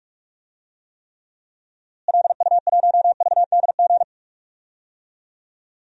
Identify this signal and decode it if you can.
Morse code, heard as a keyed tone, about 42 wpm, 700 Hz. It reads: LU1VDG
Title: DXCC-100 Call Signs - 6 Characters - Repeated 2x - 45wpm